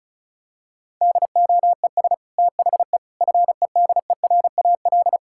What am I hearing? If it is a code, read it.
Morse code, heard as a keyed tone, about 36 wpm, 700 Hz. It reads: DOES THE FEDERAL
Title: Sentences from Top 1000 Words - Review Part 5 - 35wpm